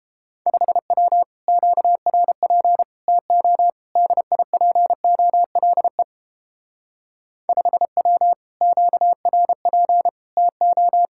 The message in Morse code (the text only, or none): 5W QRP TO DIPOLE 5W QRP TO